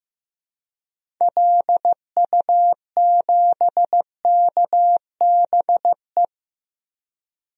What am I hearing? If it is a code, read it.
Morse code, heard as a keyed tone, about 16 wpm, 700 Hz. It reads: LU7KBE
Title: DXCC-100 Call Signs - 6 Characters - 15wpm